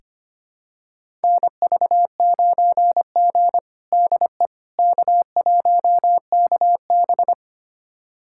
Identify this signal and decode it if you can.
Morse code, heard as a keyed tone, about 25 wpm, 700 Hz. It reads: NV9G DE K1KB
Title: USA Callsign DE Callsign - 4 Characters - 25wpm